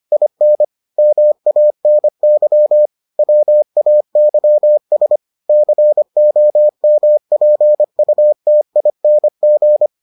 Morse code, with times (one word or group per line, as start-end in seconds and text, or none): 0.1-0.6 IN
1.0-2.9 MANY
3.2-5.2 WAYS
5.5-9.9 COMPUTING